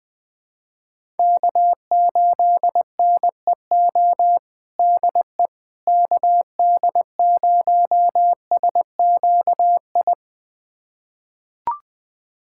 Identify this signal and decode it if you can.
Morse code, heard as a keyed tone, about 20 wpm, 700 Hz. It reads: K8NEO DE KD0SQI
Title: USA Callsign DE Callsign 20wpm